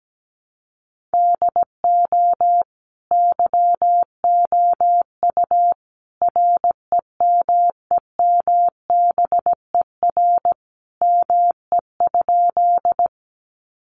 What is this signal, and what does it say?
Morse code, heard as a keyed tone, about 17 wpm, 700 Hz. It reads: DO YOU REMEMBER ME?
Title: Sentences from Top 400 Words 17wpm